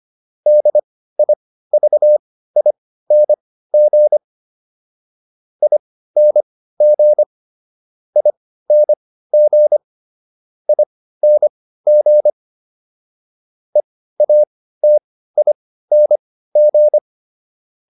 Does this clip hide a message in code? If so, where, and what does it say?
Yes: Morse code from 0.5 to 17.0 s, heard as a keyed tone, about 25 wpm, spaced as for 15 wpm, 600 Hz. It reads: DIVING ING ING ING EATING